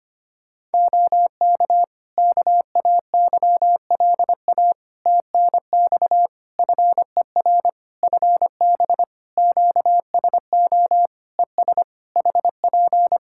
Morse code, heard as a keyed tone, about 25 wpm, 700 Hz. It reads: OK KAYLA TNX FER FB QSO ES HP
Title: K1USN SST QSO Practice 25wpm